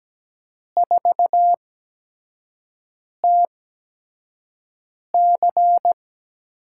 Morse code, heard as a keyed tone, about 17 wpm, spaced as for 5 wpm, 700 Hz. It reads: 4TC
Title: USA Call Signs - 4 Characters - 8x Character Spacing - 17wpm